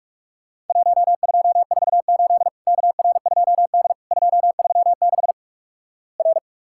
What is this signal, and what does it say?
Morse code, heard as a keyed tone, about 46 wpm, 700 Hz. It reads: 1248 KR2D 236 R
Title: Worked All Europe DX Contest - QTC Exchange - 45wpm